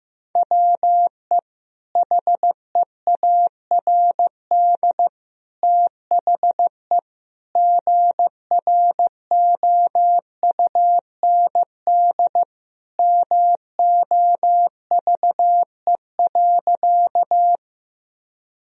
Morse code, heard as a keyed tone, about 16 wpm, 700 Hz. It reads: WE HEARD THE GROUND MOVE.
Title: Sentences from Top 1000 Words - Review Part 2 - 15wpm